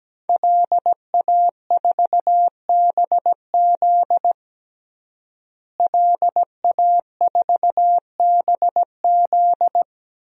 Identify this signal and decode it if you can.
Morse code, heard as a keyed tone, about 17 wpm, 700 Hz. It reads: LA4BZ LA4BZ
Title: DXCC-100 Call Signs - 5 Characters - Repeated 2x - 17wpm